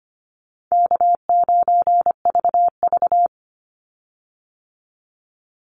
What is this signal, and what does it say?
Morse code, heard as a keyed tone, about 25 wpm, 700 Hz. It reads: K9VV
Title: Call Signs Repeated 2x - 25wpm